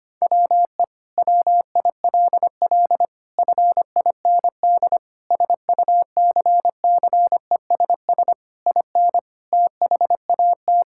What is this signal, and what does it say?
Morse code, heard as a keyed tone, about 25 wpm, 700 Hz. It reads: WE WILL FIND SUCCESS IN THAT
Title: Sentences from Top 1000 Words - Review Part 5 - 25wpm